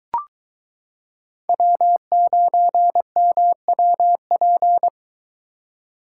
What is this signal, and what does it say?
Morse code, heard as a keyed tone, about 23 wpm, 700 Hz. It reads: W9MWP